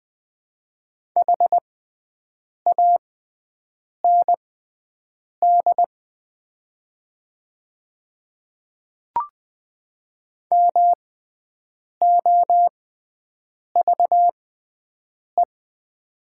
Morse code, heard as a keyed tone, about 20 wpm, spaced as for 7 wpm, 700 Hz. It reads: HAND MOVE
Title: 6x Character Spacing - Top 200 Words - 20wpm